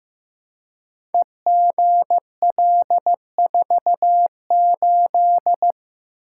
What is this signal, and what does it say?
Morse code, heard as a keyed tone, about 15 wpm, 700 Hz. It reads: EGL48
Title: Code Groups - 5 Letter:Numbers - 15wpm